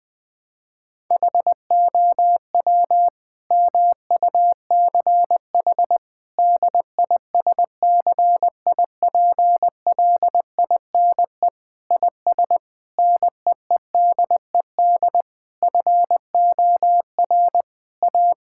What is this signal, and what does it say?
Morse code, heard as a keyed tone, about 20 wpm, 700 Hz. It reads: HOW MUCH DISCIPLINE IS NEEDED FOR A
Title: Sentences from Top 1000 Words - Review Part 6 - 20wpm